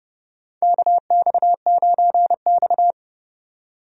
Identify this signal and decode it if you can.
Morse code, heard as a keyed tone, about 30 wpm, 700 Hz. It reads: KX9X
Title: USA Call Signs - 4 Characters - 30wpm